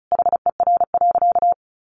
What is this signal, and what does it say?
Morse code, heard as a keyed tone, about 34 wpm, 700 Hz. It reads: HER.
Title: Sentences from Top 1100 Words 35wpm